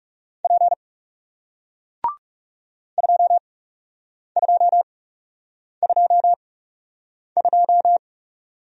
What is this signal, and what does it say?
Morse code, heard as a keyed tone, about 40 wpm, spaced as for 8 wpm, 700 Hz. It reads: P 2222